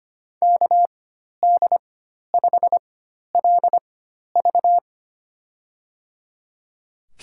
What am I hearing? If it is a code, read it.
Morse code, heard as a keyed tone, about 25 wpm, spaced as for 12 wpm, 700 Hz. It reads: KD5LV
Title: USA Call Signs - 5 Characters - 4x Character Spacing - 25wpm